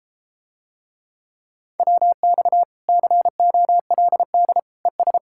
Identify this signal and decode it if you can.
Morse code, heard as a keyed tone, about 33 wpm, 700 Hz. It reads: WX COLD ES